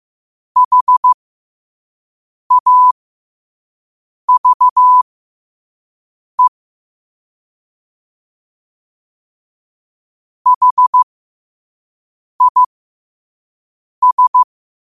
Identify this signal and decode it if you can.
Morse code, heard as a keyed tone, about 15 wpm, spaced as for 5 wpm, 1000 Hz. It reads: HAVE HIS
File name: qotdfiles / 5wpm